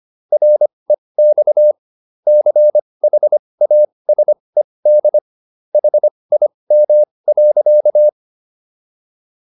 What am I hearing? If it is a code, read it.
Morse code, heard as a keyed tone, about 25 wpm, spaced as for 20 wpm, 600 Hz. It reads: REX CHASED HIM.